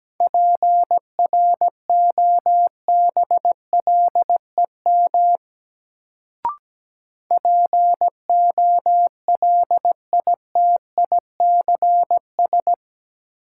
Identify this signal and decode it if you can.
Morse code, heard as a keyed tone, about 17 wpm, 700 Hz. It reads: PROBLEM POLITICS
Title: Sentences from Top 1000 Words - New Word Drill - 17wpm